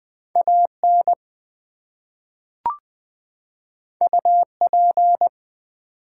Morse code, heard as a keyed tone, about 20 wpm, 700 Hz. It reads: AN UP
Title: Two Letter Words - 20wpm